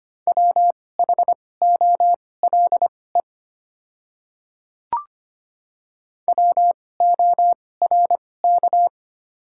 Morse code, heard as a keyed tone, about 25 wpm, spaced as for 18 wpm, 700 Hz. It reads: WHOLE WORK
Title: 2x Character Spacing - Top 500 Words - 25wpm